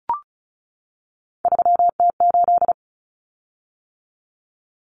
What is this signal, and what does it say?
Morse code, heard as a keyed tone, about 34 wpm, 700 Hz. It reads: 3T8
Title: Common Cut Numbers - 3-Digit 35wpm